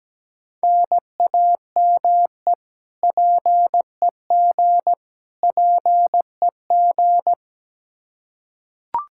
Morse code, heard as a keyed tone, about 17 wpm, 700 Hz. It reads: NAME PEG PEG
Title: Practice QSO - Exchange 1 - Beginning - 17wpm